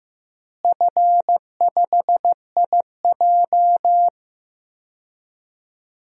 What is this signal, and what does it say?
Morse code, heard as a keyed tone, about 15 wpm, 700 Hz. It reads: F5IJ
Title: DXCC-100 Call Signs - 4 Characters - Repeated 2x - 15wpm